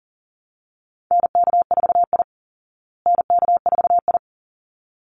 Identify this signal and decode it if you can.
Morse code, heard as a keyed tone, about 37 wpm, 700 Hz. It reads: NK4I NK4I
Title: Call Signs Repeated 3x - 40wpm